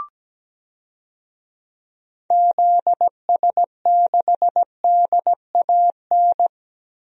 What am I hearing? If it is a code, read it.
Morse code, heard as a keyed tone, about 17 wpm, 700 Hz. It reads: ZS6DAN
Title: DXCC-100 Call Signs - 6 Characters - Repeated 3x - 17wpm